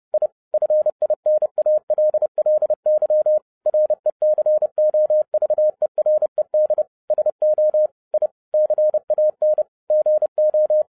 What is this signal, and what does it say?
Morse code, heard as a keyed tone, about 30 wpm, 610 Hz. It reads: I FINALLY RECOVERED SO I CAN GO
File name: 2011-07-mp3 / n60 BACK FROM A BIG ACCIDENT 30wpm